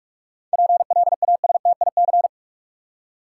Morse code, heard as a keyed tone, about 47 wpm, 700 Hz. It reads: PLASTIC
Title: Seven Letter Words - 45wpm